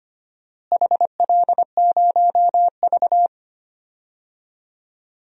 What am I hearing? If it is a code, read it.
Morse code, heard as a keyed tone, about 25 wpm, 700 Hz. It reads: HL0V
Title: DXCC-100 Call Signs - 4 Characters - Repeated 3x - 25wpm